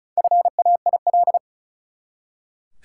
Morse code, heard as a keyed tone, about 36 wpm, 700 Hz. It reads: FAIL
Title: Four Letter Words - 35wpm